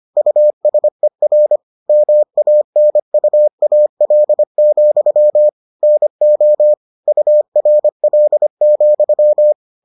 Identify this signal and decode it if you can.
Morse code, heard as a keyed tone, about 25 wpm, 600 Hz. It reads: USER MANUAL, NO URL,